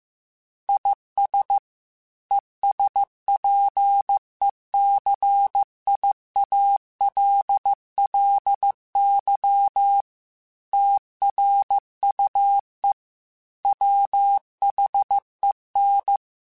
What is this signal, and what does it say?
Morse code, heard as a keyed tone, about 15 wpm, 800 Hz. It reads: IS ESPECIALLY TRUE WHEN
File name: Technical Subjects / Technical Subjects-mp3s / AND Gate @13